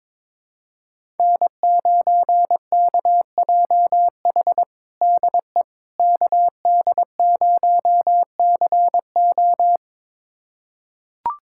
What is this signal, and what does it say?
Morse code, heard as a keyed tone, about 21 wpm, 700 Hz. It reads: N9KJH DE KD0CO
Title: USA Callsign DE Callsign - 5 Characters - 22wpm